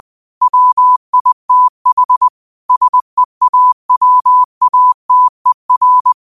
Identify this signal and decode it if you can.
Morse code, heard as a keyed tone, about 20 wpm, 1000 Hz. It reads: WITH SEAWATER